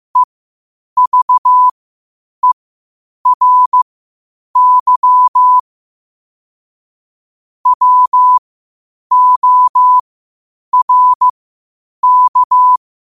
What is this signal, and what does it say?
Morse code, heard as a keyed tone, about 15 wpm, spaced as for 8 wpm, 1000 Hz. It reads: EVERY WORK
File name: qotdfiles / 8wpm